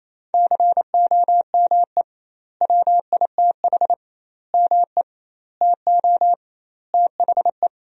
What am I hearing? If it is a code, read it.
Morse code, heard as a keyed tone, about 28 wpm, 700 Hz. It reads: COME WITH ME TO THE
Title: 2x Word-Spacing - Sentences from Top 200 Words - 28wpm